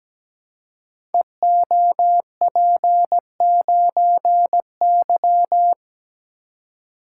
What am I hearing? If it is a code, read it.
Morse code, heard as a keyed tone, about 17 wpm, 700 Hz. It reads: EOP9Y
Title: Code Groups - 5 Letter:Numbers - 17wpm